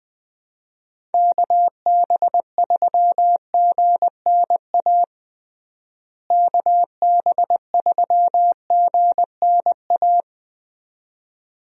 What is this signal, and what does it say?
Morse code, heard as a keyed tone, about 20 wpm, 700 Hz. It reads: KB3GNA KB3GNA
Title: USA Call Signs - 6 Characters - Repeated 2x - 20wpm